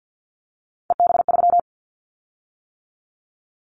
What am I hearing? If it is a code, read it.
Morse code, heard as a keyed tone, about 50 wpm, 700 Hz. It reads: EB3
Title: Callsign Prefixes for Top 25 DXCC - 50wpm